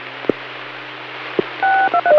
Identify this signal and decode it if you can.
DTMF, heard as telephone touch tones, tones of about 80 ms. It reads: 623